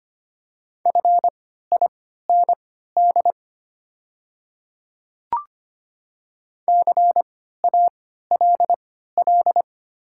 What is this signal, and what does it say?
Morse code, heard as a keyed tone, about 25 wpm, spaced as for 14 wpm, 700 Hz. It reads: FIND CALL
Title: 3x Character Spacing - Top 200 Words - 25wpm